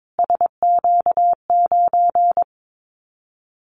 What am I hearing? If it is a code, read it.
Morse code, heard as a keyed tone, about 22 wpm, 700 Hz. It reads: SQ9
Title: Callsign Prefixes for Top 25 DXCC - Repeated 3x - 22wpm